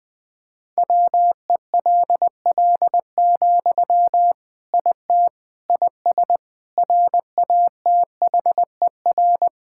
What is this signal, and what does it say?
Morse code, heard as a keyed tone, about 21 wpm, 700 Hz. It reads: WELL, IT IS RATHER
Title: Sentences from Top 1600 Words 20wpm